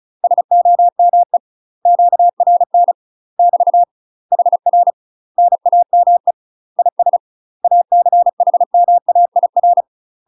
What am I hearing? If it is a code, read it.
Morse code, heard as a keyed tone, about 34 wpm, 700 Hz. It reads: SOME QRN <BT> HR NAME IS ACHMAIR
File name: UK MASTER QSOs TRAINING / 35WPM / UK MASTER QSOs 0005